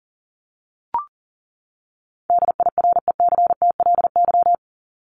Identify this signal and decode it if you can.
Morse code, heard as a keyed tone, about 40 wpm, 700 Hz. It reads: DIRECTLY
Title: Top 501-750 Words 40wpm